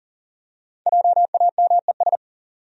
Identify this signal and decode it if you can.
Morse code, heard as a keyed tone, about 38 wpm, 700 Hz. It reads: JAMES
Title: USA Common Names - Max 5 Letters - 40wpm